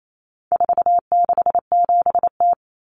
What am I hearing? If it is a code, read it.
Morse code, heard as a keyed tone, about 28 wpm, 700 Hz. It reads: 467T
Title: Common Cut Numbers - 4-Digit 28wpm